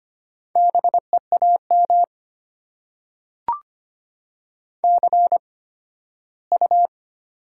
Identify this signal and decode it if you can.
Morse code, heard as a keyed tone, about 25 wpm, spaced as for 15 wpm, 690 Hz. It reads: BEAM CU
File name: Top 90 QSO Elements - 8x Character Spacing - 25wpm